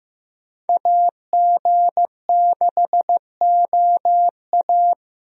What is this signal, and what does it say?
Morse code, heard as a keyed tone, about 15 wpm, 700 Hz. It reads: AG6OA
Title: USA Call Signs - 5 Characters - Repeated 2x - 15wpm